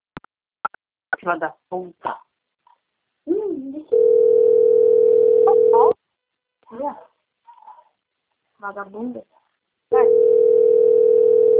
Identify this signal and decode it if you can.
Ringback tone (North America), 440 + 480 Hz, with speech.
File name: rg-1600-unknown- - - .12407